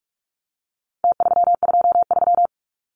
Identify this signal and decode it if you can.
Morse code, heard as a keyed tone, about 44 wpm, 700 Hz. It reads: T323
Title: Numbers - Contest Style - 4-Digit 45wpm